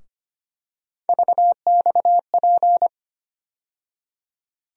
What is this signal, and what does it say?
Morse code, heard as a keyed tone, about 25 wpm, 700 Hz. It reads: VXP
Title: USA Call Sign Postfixes - 3 Letters 25wpm